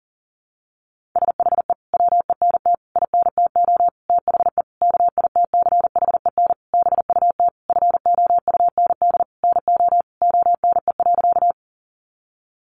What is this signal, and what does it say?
Morse code, heard as a keyed tone, about 40 wpm, 700 Hz. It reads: SHE WENT INTO THE KITCHEN BUT FOUND NO ONE.
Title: Sentences from Top 1400 Words 40wpm